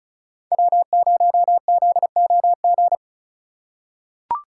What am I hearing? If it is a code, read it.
Morse code, heard as a keyed tone, about 36 wpm, 700 Hz. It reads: W0ZOG
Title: USA Call Signs - 5 Characters - 2x Character Spacing - 35wpm